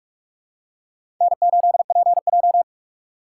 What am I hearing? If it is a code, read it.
Morse code, heard as a keyed tone, about 46 wpm, 700 Hz. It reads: N8PJ
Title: USA Call Signs - 4 Characters - 3x Character Spacing - 45wpm